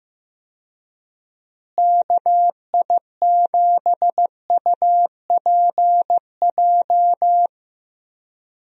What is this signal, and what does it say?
Morse code, heard as a keyed tone, about 15 wpm, 700 Hz. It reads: KI7UPJ
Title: USA Call Signs - 6 Characters - Repeated 2x - 15wpm